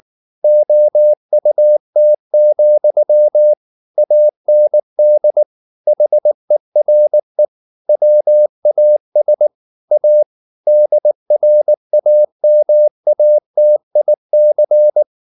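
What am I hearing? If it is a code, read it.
Morse code, heard as a keyed tone, about 19 wpm, 600 Hz. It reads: OUT, AND HERE WAS A DRAMATIC
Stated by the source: Morse code at 19 wpm from CW med Gnister, reading Ken Follett's novel Never in English.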